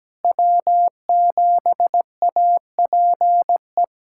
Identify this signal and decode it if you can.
Morse code, heard as a keyed tone, about 17 wpm, 700 Hz. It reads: W7APE
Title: USA Callsign DE Callsign 17wpm